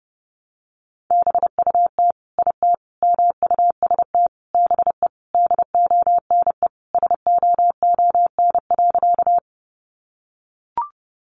Morse code, heard as a keyed tone, about 28 wpm, 700 Hz. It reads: BUT IT MUST BE DONE SOON.
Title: Sentences from Top 500 Words 30wpm